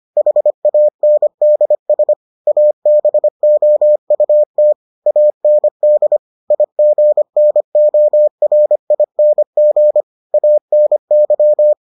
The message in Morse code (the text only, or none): HANDS ABOUT AND IGNORING ANY